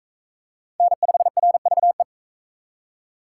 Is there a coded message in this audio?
Yes: Morse code from 0.8 to 2.0 s, heard as a keyed tone, about 44 wpm, 700 Hz. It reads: N5RVE